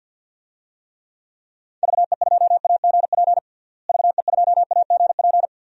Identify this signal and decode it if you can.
Morse code, heard as a keyed tone, about 53 wpm, 700 Hz. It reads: VE2AGP VE2AGP
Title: DXCC-100 Call Signs - 6 Characters - Repeated 2x - 50wpm